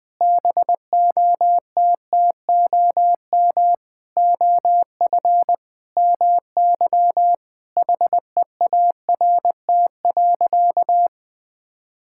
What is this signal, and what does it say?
Morse code, heard as a keyed tone, about 20 wpm, 700 Hz. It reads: BOTTOM OF MY HEART.
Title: Sentences with 8-Word Phrases - 20wpm